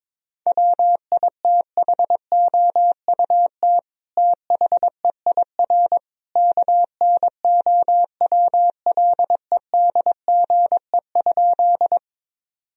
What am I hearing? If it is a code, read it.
Morse code, heard as a keyed tone, about 22 wpm, 700 Hz. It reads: WITHOUT THEIR KNOWLEDGE?
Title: Sentences from Top 1000 Words - Review Part 3 - 22wpm